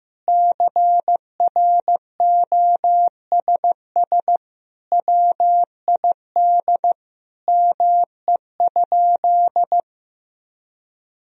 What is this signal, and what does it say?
Morse code, heard as a keyed tone, about 15 wpm, 700 Hz. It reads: CROSS WID ME?
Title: Sentences from Top 300 Words and Abbreviations 15wpm